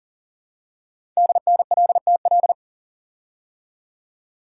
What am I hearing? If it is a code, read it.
Morse code, heard as a keyed tone, about 39 wpm, 700 Hz. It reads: DNLTL
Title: Code Groups - 5 Letters - 40wpm